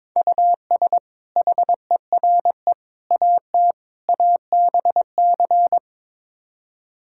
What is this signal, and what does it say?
Morse code, heard as a keyed tone, about 22 wpm, 700 Hz. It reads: US HERE AT ABC
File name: practice / 22wpm / Build A Phrase - 5-Word Phrases - 22wpm